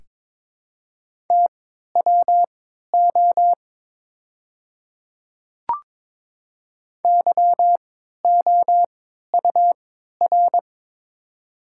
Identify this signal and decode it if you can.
Morse code, heard as a keyed tone, about 23 wpm, spaced as for 13 wpm, 700 Hz. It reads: TWO YOUR
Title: 3x Character Spacing - Top 200 Words - 22wpm